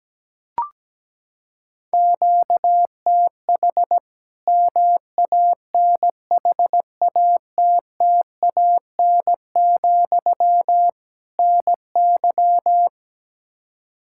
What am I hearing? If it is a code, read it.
Morse code, heard as a keyed tone, about 17 wpm, 700 Hz. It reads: QTH MANHATTAN, NY
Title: Practice QSO - Exchange 1 - Beginning - 17wpm